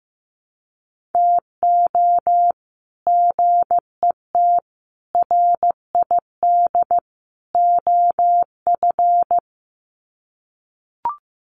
Morse code, heard as a keyed tone, about 15 wpm, 700 Hz. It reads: TO GET RID OF
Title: Sets of 4 Words 15wpm